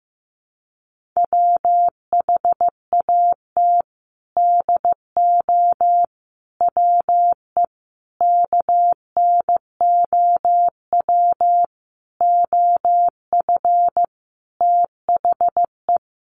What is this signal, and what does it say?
Morse code, heard as a keyed tone, about 15 wpm, 700 Hz. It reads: WHAT DO WE KNOW OF THE
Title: Sentences from Top 1400 Words 15wpm